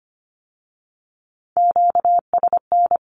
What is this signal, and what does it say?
Morse code, heard as a keyed tone, about 25 wpm, 700 Hz. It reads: QSN
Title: Q-Codes with Spoken Meaning - 25wpm